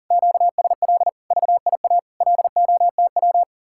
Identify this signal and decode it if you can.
Morse code, heard as a keyed tone, about 50 wpm, 700 Hz. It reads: QSL VIA LOTW
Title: Speed-Racing Common QSO Phrases Encore - 40wpm